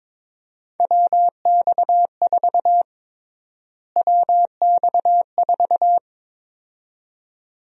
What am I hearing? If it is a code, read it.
Morse code, heard as a keyed tone, about 22 wpm, 700 Hz. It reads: WX4 WX4
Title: USA Call Sign Prefixes - Repeated 2x - 22wpm